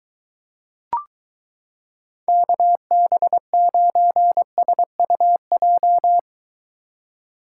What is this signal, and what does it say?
Morse code, heard as a keyed tone, about 23 wpm, 700 Hz. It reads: KB9SUJ